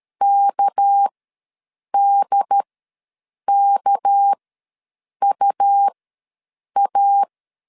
Morse code, heard as a keyed tone, about 13 wpm, spaced as for 7 wpm, 800 Hz. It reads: KDKUA
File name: LEZIONI / 6 PKX0 / 6 PK ws 12wpm 3116